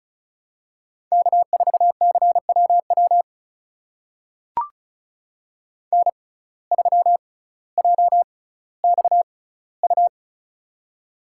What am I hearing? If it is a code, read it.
Morse code, heard as a keyed tone, about 34 wpm, spaced as for 18 wpm, 700 Hz. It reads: K4CWW N3JXU
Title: USA Call Signs - 5 Characters - 6x Character Spacing - 35wpm